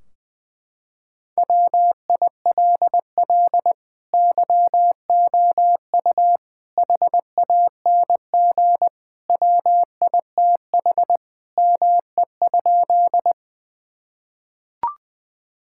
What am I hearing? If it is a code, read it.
Morse code, heard as a keyed tone, about 19 wpm, 700 Hz. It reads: WILL YOU HANG WITH ME?
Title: Sentences from Top 1600 Words 20wpm